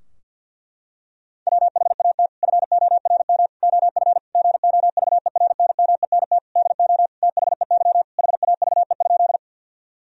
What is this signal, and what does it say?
Morse code, heard as a keyed tone, about 51 wpm, 700 Hz. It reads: WHAT FORM OF GOVERNMENT DO THEY HAVE?
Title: Sentences from Top 1000 Words - Review Part 3 - 50wpm